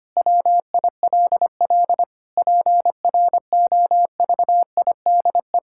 Morse code, heard as a keyed tone, about 25 wpm, 700 Hz. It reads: WILL PROVIDE
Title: Sentences from Top 1000 Words - Review Part 6 - 25wpm